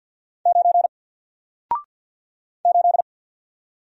Morse code, heard as a keyed tone, about 48 wpm, 700 Hz. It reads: 9 8
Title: Single Letter-Number Alt 50wpm